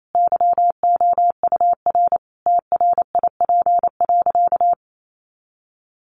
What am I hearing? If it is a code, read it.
Morse code, heard as a keyed tone, about 28 wpm, 700 Hz. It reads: YOUR TRIP.
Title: Sentences from Top 1500 Words 28wpm